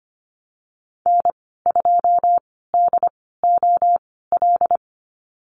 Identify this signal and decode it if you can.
Morse code, heard as a keyed tone, about 25 wpm, spaced as for 16 wpm, 700 Hz. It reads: N2DOL
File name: Lesson 15g - Number 5 - Call Signs 25wpm